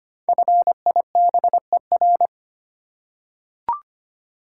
Morse code, heard as a keyed tone, about 25 wpm, 700 Hz. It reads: FIBER
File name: 5 Letter Words - First 3 Letters Given - 25wpm